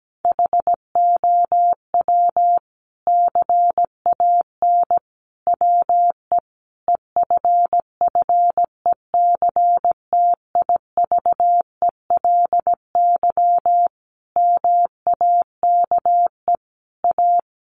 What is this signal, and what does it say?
Morse code, heard as a keyed tone, about 17 wpm, 700 Hz. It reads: HOW CAN WE EFFECTIVELY MAKE A
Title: Sentences from Top 1400 Words 17wpm